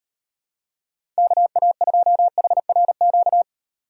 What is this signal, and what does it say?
Morse code, heard as a keyed tone, about 38 wpm, 700 Hz. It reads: KA2HRQ